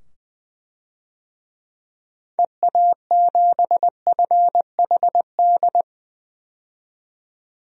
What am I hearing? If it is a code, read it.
Morse code, heard as a keyed tone, about 20 wpm, 700 Hz. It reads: EA7FHD
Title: DXCC-100 Call Signs - 6 Characters - 20wpm